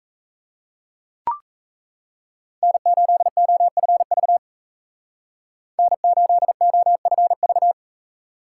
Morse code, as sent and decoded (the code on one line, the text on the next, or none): -. ---.. --- ..-. ...- / -. ---.. --- ..-. ...-
N8OFV N8OFV